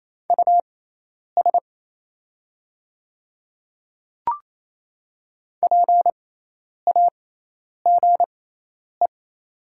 Morse code, heard as a keyed tone, about 28 wpm, spaced as for 10 wpm, 700 Hz. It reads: US PAGE
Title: 6x Character Spacing - Top 500 Words - 28wpm